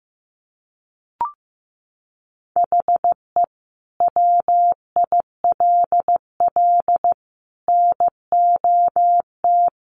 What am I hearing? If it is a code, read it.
Morse code, heard as a keyed tone, about 15 wpm, 700 Hz. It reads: HE WILL NOT
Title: Sentences from Top 600 Words 15wpm